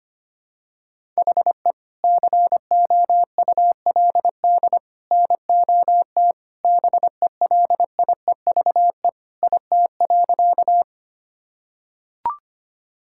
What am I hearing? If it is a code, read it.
Morse code, heard as a keyed tone, about 25 wpm, 700 Hz. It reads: HE COULD NOT BELIEVE IT.
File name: practice / 25wpm / Sentences with 3-Word Phrases - 25wpm